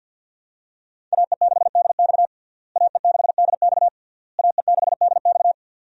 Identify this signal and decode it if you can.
Morse code, heard as a keyed tone, about 50 wpm, 700 Hz. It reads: AE6DX AE6DX AE6DX
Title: USA Call Signs - 5 Characters - Repeated 3x - 50wpm